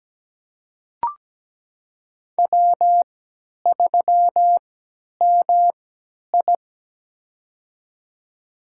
Morse code, heard as a keyed tone, about 17 wpm, spaced as for 10 wpm, 700 Hz. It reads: W3MI
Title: USA Call Signs - 4 Characters - 3x Character Spacing - 17wpm